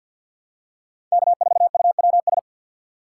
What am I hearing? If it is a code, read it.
Morse code, heard as a keyed tone, about 46 wpm, 700 Hz. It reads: K4UWS
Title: USA Call Signs - 5 Characters - 4x Character Spacing - 50wpm